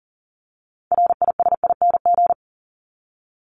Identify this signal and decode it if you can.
Morse code, heard as a keyed tone, about 40 wpm, 700 Hz. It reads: RISING
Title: ING Suffix - Top 100 Words with Base Words 40wpm